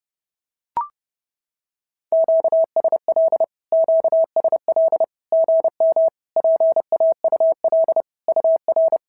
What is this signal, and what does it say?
Morse code, heard as a keyed tone, about 30 wpm, 650 Hz. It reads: QSL QSL GM PAUL UR
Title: SOTA QSO Practice 30wpm